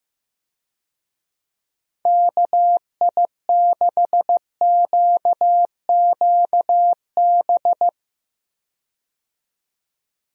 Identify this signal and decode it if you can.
Morse code, heard as a keyed tone, about 15 wpm, 700 Hz. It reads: KI6QQB
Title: USA Call Signs - 6 Characters - 15wpm